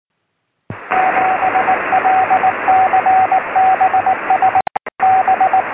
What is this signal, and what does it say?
Morse code, heard as a keyed tone, about 18 wpm, 760 Hz, with radio noise. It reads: 7LCB S B